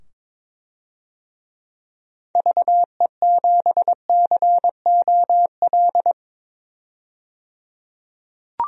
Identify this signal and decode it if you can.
Morse code, heard as a keyed tone, about 22 wpm, 700 Hz. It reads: VE7COL
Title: DXCC-100 Call Signs - 6 Characters - 22wpm